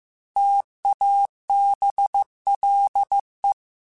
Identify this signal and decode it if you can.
Morse code, heard as a keyed tone, about 15 wpm, 800 Hz. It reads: TABLE
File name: Technical Subjects / Technical Subjects-mp3s / AND Gate @13